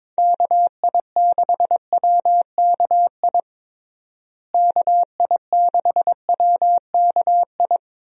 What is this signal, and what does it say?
Morse code, heard as a keyed tone, about 22 wpm, 700 Hz. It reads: KI6WKI KI6WKI
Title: USA Call Signs - 6 Characters - Repeated 3x - 22wpm